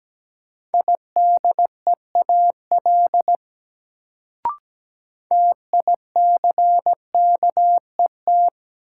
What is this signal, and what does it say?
Morse code, heard as a keyed tone, about 17 wpm, 700 Hz. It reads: IDEAL TICKET
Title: Sentences from Top 1400 Words - New Word Drill Alt - 17wpm